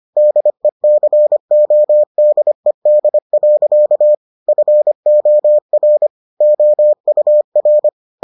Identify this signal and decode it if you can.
Morse code, heard as a keyed tone, about 25 wpm, 600 Hz. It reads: DECODED. FOR OUR